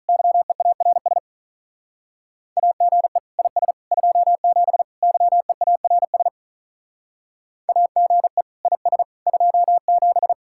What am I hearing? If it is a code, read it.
Morse code, heard as a keyed tone, about 45 wpm, 700 Hz. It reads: YEARS AGE IS 27 YEARS AGE IS 27